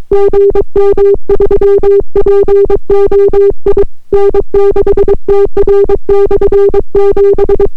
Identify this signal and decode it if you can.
Morse code, heard as a keyed tone, about 22 wpm, 400 Hz. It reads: GM3POI N6TR/7